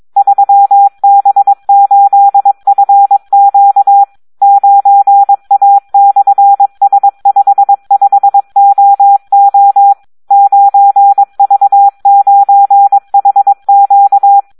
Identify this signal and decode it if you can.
Morse code, heard as a keyed tone, about 22 wpm, 800 Hz. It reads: 3B8FQ 9A/S55OO 9V9HQ
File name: Audio practica CW / SD 1 22